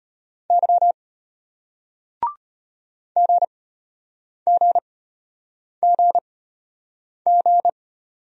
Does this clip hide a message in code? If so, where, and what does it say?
Yes: Morse code from 0.5 to 7.7 s, heard as a keyed tone, about 33 wpm, spaced as for 8 wpm, 700 Hz. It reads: Y GGGG